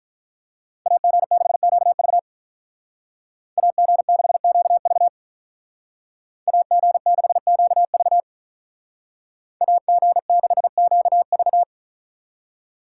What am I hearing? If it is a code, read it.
Morse code, heard as a keyed tone, about 43 wpm, 700 Hz. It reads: AG6QV AG6QV AG6QV AG6QV
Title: Speed Racing - USA 2x2 Call Signs 35wpm